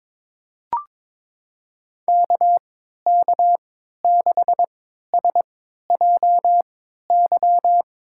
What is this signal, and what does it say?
Morse code, heard as a keyed tone, about 21 wpm, spaced as for 12 wpm, 700 Hz. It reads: KK6SJY
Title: USA Call Signs - 6 Characters - 3x Character Spacing - 22wpm